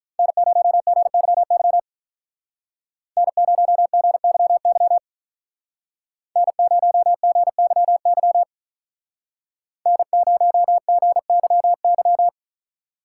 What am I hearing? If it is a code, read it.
Morse code, heard as a keyed tone, about 46 wpm, 700 Hz. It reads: N0GYY N0GYY N0GYY N0GYY